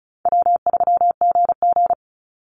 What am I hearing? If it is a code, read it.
Morse code, heard as a keyed tone, about 34 wpm, 700 Hz. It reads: W3GG
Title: Call Signs Repeated 3x - 35wpm